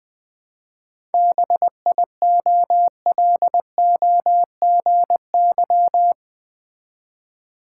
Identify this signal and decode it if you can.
Morse code, heard as a keyed tone, about 20 wpm, 700 Hz. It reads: BIOLOGY
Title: 7 Letter Words - First 3 Letters Given - 20wpm